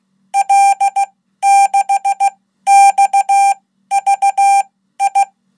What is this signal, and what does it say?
Morse code, heard as a keyed tone, about 16 wpm, spaced as for 12 wpm, 790 Hz. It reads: L6XVI
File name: Code